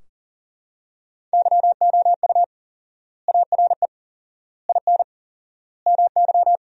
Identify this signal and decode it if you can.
Morse code, heard as a keyed tone, about 40 wpm, 700 Hz. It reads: YOU ARE IN MY
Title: 4x Word-Spacing - Sentences from Top 500 Words - 40wpm